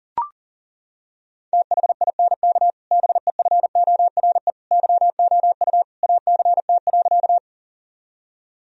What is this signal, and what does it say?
Morse code, heard as a keyed tone, about 37 wpm, 700 Hz. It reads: THINK BEFORE YOU ACT.
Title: Sentences from Top 1000 Words - Review Part 1 - 40wpm